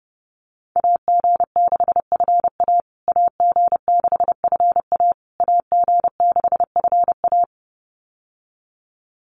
Morse code, heard as a keyed tone, about 30 wpm, 700 Hz. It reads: AG6FA AG6FA AG6FA
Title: USA Call Signs Repeated 3x 30wpm